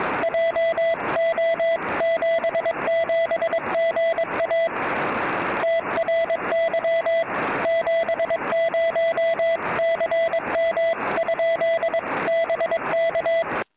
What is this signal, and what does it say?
Morse code, heard as a keyed tone, about 22 wpm, 660 Hz, with noise. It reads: JO77GA TRY 70CM? BK
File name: tropo / sm7gvf